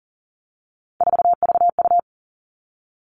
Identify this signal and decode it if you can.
Morse code, heard as a keyed tone, about 40 wpm, 700 Hz. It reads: 4VU
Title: All Cut Numbers - 3-Digit 40wpm